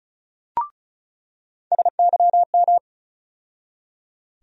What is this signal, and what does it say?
Morse code, heard as a keyed tone, about 36 wpm, 700 Hz. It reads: SYM
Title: USA Call Sign Postfixes - 3 Letters 35wpm